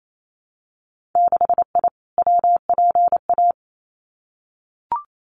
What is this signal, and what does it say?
Morse code, heard as a keyed tone, about 28 wpm, 700 Hz. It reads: 6I WPA
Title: Winter Field Day Exchanges Encore 28wpm